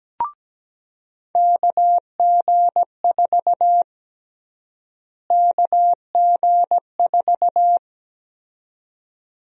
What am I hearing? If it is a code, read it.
Morse code, heard as a keyed tone, about 17 wpm, 700 Hz. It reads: KG4 KG4
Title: USA Call Sign Prefixes - Repeated 2x - 17wpm